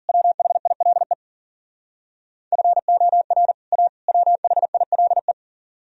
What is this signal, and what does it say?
Morse code, heard as a keyed tone, about 51 wpm, 700 Hz. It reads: WHILE FOR A WHILE